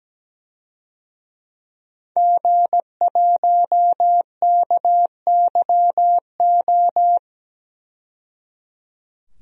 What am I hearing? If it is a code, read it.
Morse code, heard as a keyed tone, about 17 wpm, 700 Hz. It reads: G1KYO
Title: DXCC-100 Call Signs - 5 Characters - 17wpm